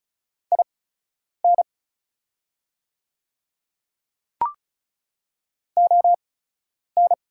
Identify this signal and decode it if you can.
Morse code, heard as a keyed tone, about 36 wpm, spaced as for 10 wpm, 700 Hz. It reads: IN ON
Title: 8x Character Spacing - Top 100 Words - 35wpm